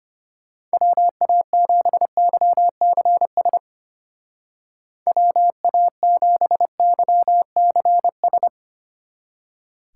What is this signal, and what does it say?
Morse code, heard as a keyed tone, about 33 wpm, 700 Hz. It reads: WA7YCS WA7YCS